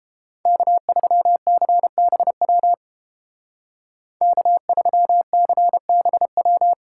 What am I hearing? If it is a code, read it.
Morse code, heard as a keyed tone, about 31 wpm, 700 Hz. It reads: K3CBW K3CBW